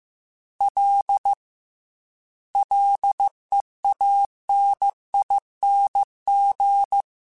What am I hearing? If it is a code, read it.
Morse code, heard as a keyed tone, about 15 wpm, 800 Hz. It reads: L LEANING
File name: Combinations / Combinations-mp3s / Combinations@13